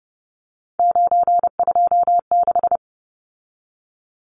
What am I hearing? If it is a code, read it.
Morse code, heard as a keyed tone, about 30 wpm, 700 Hz. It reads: 926
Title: Numbers - 3-Digit 30wpm